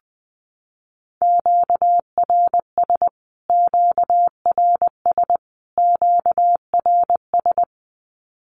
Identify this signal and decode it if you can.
Morse code, heard as a keyed tone, about 20 wpm, 700 Hz. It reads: QRS QRS QRS
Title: Q-Codes Repeated 3x 20wpm